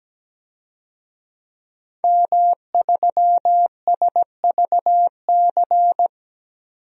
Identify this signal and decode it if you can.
Morse code, heard as a keyed tone, about 17 wpm, 700 Hz. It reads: M3SVC
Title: DXCC-100 Call Signs - 5 Characters - 17wpm